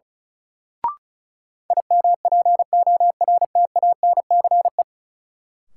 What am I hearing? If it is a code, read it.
Morse code, heard as a keyed tone, about 36 wpm, 700 Hz. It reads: IMPORTANCE
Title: Sentences from Top 1200 Words - New Word Drill - 35wpm